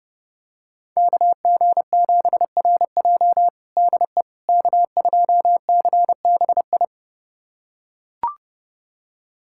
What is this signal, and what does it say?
Morse code, heard as a keyed tone, about 30 wpm, 700 Hz. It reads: KG7RJ DE K2CBI
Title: USA Callsign DE Callsign - 5 Characters - 30wpm